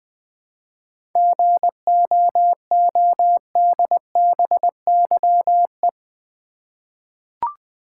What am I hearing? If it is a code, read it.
Morse code, heard as a keyed tone, about 20 wpm, 700 Hz. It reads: GOODBYE
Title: 7 Letter Words - First 3 Letters Given - 20wpm